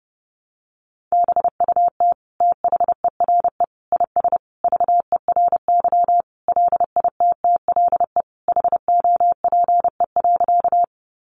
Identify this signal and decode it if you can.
Morse code, heard as a keyed tone, about 30 wpm, 700 Hz. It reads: BUT THERE IS VERY LITTLE HOPE.
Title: Sentences from Top 1000 Words 30wpm